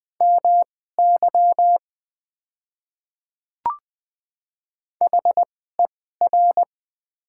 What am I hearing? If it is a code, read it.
Morse code, heard as a keyed tone, about 20 wpm, spaced as for 14 wpm, 700 Hz. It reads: MY HER
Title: 2x Character Spacing - Top 100 Words - 20wpm